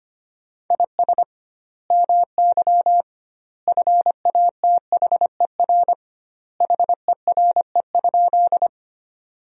Morse code, heard as a keyed tone, about 25 wpm, 700 Hz. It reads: IS MY FATHER HERE?
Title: 2x Word-Spacing - Sentences from Top 300 Words - 25wpm